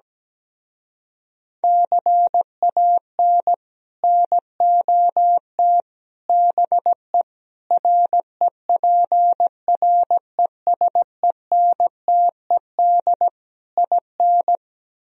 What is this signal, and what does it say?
Morse code, heard as a keyed tone, about 17 wpm, 700 Hz. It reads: CAN NOT BE REPRESENTED IN
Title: Speed-Racing - Sets of 5 Words 17wpm